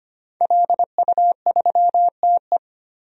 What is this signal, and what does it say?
Morse code, heard as a keyed tone, about 25 wpm, 700 Hz. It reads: LU3TE
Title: DXCC-100 Call Signs - 5 Characters - 25wpm